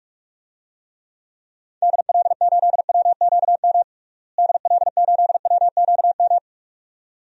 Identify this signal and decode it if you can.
Morse code, heard as a keyed tone, about 42 wpm, 690 Hz. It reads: DL8WQM DL8WQM
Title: DXCC-100 Call Signs - 6 Characters - Repeated 2x - 45wpm